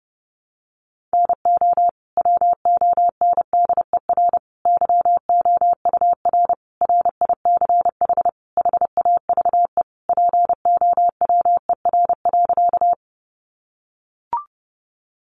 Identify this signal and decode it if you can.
Morse code, heard as a keyed tone, about 30 wpm, 700 Hz. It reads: NO WONDER YOUR RICH HAVE POWER.
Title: Sentences from Top 1100 Words 30wpm